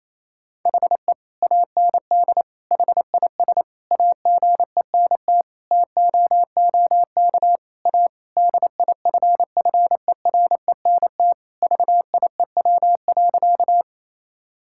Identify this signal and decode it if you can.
Morse code, heard as a keyed tone, about 28 wpm, 700 Hz. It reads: HE AND HIS AGENT TOOK A DIFFERENT VIEW.
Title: Sentences from Top 1600 Words 28wpm